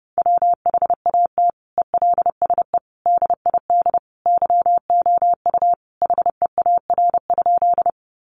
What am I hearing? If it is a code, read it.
Morse code, heard as a keyed tone, about 30 wpm, 700 Hz. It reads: WHAT ELSE DID YOU HEAR?
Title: Sentences from Top 800 Words 30wpm